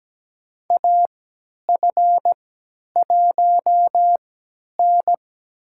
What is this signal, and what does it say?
Morse code, heard as a keyed tone, about 17 wpm, spaced as for 10 wpm, 700 Hz. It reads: AF1N
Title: USA Call Signs - 3x Character Spacing - 17wpm